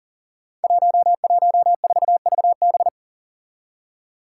Morse code, heard as a keyed tone, about 41 wpm, 700 Hz. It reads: 114VB
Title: Five Challenging Letter-Number Combinations 40wpm